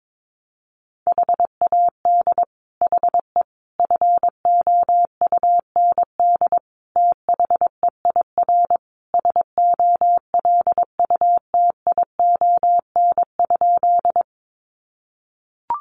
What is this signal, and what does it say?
Morse code, heard as a keyed tone, about 22 wpm, 700 Hz. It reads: HAD HE FOUND THEIR SOLUTION?
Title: Sentences from Top 1300 Words 22wpm